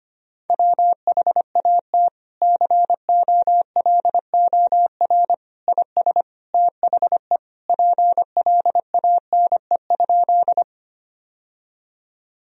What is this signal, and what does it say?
Morse code, heard as a keyed tone, about 25 wpm, 700 Hz. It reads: WHAT COLOR IS THE PLANE?
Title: Sentences from Top 1000 Words - Review Part 3 - 25wpm